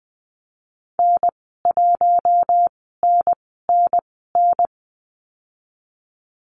Morse code, heard as a keyed tone, about 20 wpm, spaced as for 14 wpm, 700 Hz. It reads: N1NNN
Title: 069 Lesson 12g - Letter d - Call Signs 20wpm